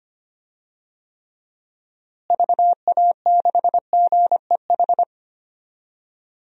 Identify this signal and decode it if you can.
Morse code, heard as a keyed tone, about 25 wpm, 700 Hz. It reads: VA6GEH
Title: DXCC-100 Call Signs - 6 Characters - Repeated 3x - 25wpm